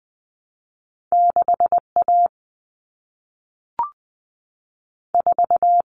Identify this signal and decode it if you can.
Morse code, heard as a keyed tone, about 20 wpm, 700 Hz. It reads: 6A 4
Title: Common Cut Numbers - 2-Digit 20wpm